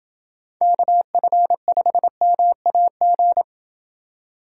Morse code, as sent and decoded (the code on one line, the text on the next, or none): -.- ..-. ..... -- .- --.
KF5MAG